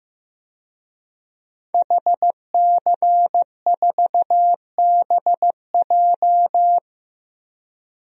Morse code, heard as a keyed tone, about 15 wpm, 700 Hz. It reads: HC4BJ
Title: DXCC-100 Call Signs - 5 Characters - Repeated 2x - 15wpm